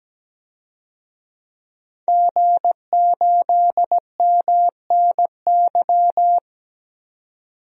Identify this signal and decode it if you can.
Morse code, heard as a keyed tone, about 17 wpm, 700 Hz. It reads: G8MNY
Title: DXCC-100 Call Signs - 5 Characters - 17wpm